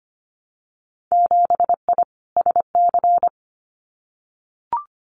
Morse code, heard as a keyed tone, about 25 wpm, 700 Hz. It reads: 7I SC
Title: Winter Field Day Exchanges Encore 25wpm